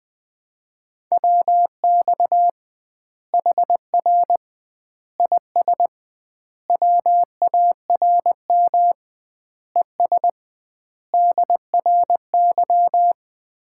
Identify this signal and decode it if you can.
Morse code, heard as a keyed tone, about 20 wpm, 700 Hz. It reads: WX HR IS WARM ES DRY
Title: Common QSO Phrases - 2x Word-Spacing - 20wpm